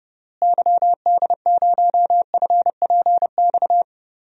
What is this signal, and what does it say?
Morse code, heard as a keyed tone, about 31 wpm, 700 Hz. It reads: YD0FPX
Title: DXCC-100 Call Signs - 6 Characters - Repeated 3x - 30wpm